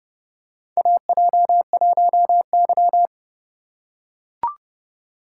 Morse code, heard as a keyed tone, about 30 wpm, 690 Hz. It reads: AJ1Y
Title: USA Call Signs - 4 Characters - 6x Character Spacing - 30wpm